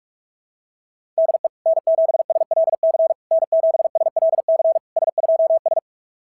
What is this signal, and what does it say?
Morse code, heard as a keyed tone, about 44 wpm, 660 Hz. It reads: DE N7SLC N7SLC S2S
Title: SOTA QSO Practice 45wpm